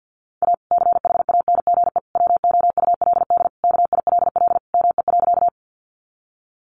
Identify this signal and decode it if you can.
Morse code, heard as a keyed tone, about 50 wpm, 700 Hz. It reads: A CHANGE WOULD KILL ME.
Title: Sentences from Top 1400 Words 50wpm